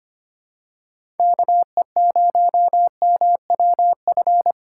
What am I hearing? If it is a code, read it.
Morse code, heard as a keyed tone, about 25 wpm, 700 Hz. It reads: KE0MWF
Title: USA Call Signs - 6 Characters - 4x Character Spacing - 25wpm